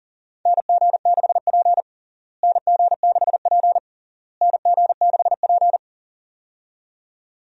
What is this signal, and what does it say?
Morse code, heard as a keyed tone, about 40 wpm, 700 Hz. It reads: NG6P NG6P NG6P
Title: USA Call Signs - 4 Characters - Repeated 3x - 40wpm